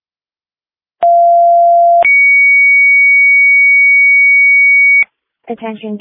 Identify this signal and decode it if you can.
Two-tone page, heard as a one-second tone, then a three-second tone, about 690 then 2250 Hz, with speech. It is heard from 1.0 to 5.0 s.